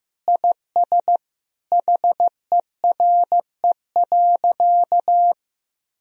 Morse code, heard as a keyed tone, about 15 wpm, 700 Hz. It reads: IS HERE.
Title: Sentences from Top 1000 Words - Review Part 2 - 15wpm